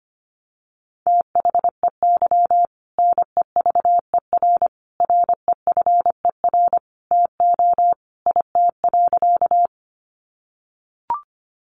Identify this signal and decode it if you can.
Morse code, heard as a keyed tone, about 25 wpm, 700 Hz. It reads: THEY NEVER REFER TO IT.
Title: Sentences from Top 1000 Words 25wpm